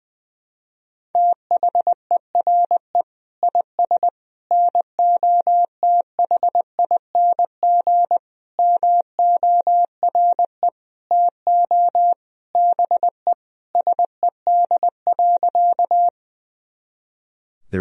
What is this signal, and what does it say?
Morse code, heard as a keyed tone, about 20 wpm, 700 Hz. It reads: THERE IS NOTHING MORE TO BE SED.
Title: Sentences from Top 500 Words and Abbreviations 20wpm